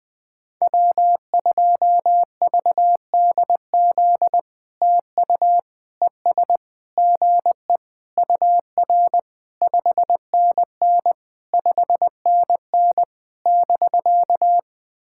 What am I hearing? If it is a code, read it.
Morse code, heard as a keyed tone, about 20 wpm, 700 Hz. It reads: W2VDZ TU ES GE UR 5NN 5NN <BK>
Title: POTA QSO Practice - 20wpm